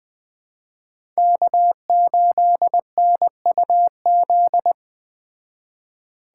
Morse code, heard as a keyed tone, about 21 wpm, 700 Hz. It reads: K8NUZ
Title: Speed Racing - USA 1x3 Call Signs 17wpm